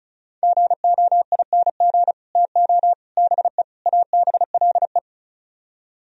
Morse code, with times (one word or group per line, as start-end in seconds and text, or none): 0.4-2.1 GOING
2.4-2.9 TO
3.2-3.6 BE
3.9-5.0 ABLE